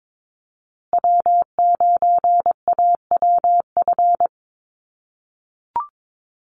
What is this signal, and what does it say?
Morse code, heard as a keyed tone, about 22 wpm, 700 Hz. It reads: W9AWF
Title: USA Call Signs Repeated 2x 22wpm